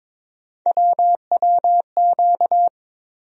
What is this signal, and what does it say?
Morse code, heard as a keyed tone, about 22 wpm, 700 Hz. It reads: WWQ
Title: USA Call Sign Postfixes - 3 Letters 22wpm